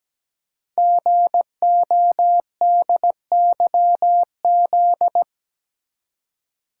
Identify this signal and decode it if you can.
Morse code, heard as a keyed tone, about 17 wpm, 700 Hz. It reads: GODYZ
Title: Code Groups - 5 Letters - 17wpm